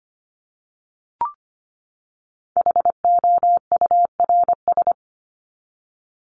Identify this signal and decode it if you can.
Morse code, heard as a keyed tone, about 25 wpm, 700 Hz. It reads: HOURS
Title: ES:S Suffix - Top 100 Words with Base Words 25wpm